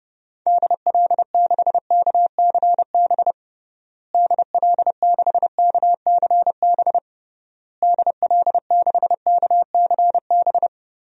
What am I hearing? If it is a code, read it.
Morse code, heard as a keyed tone, about 30 wpm, 700 Hz. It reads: DL6KCB DL6KCB DL6KCB
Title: DXCC-100 Call Signs - 6 Characters - Repeated 3x - 30wpm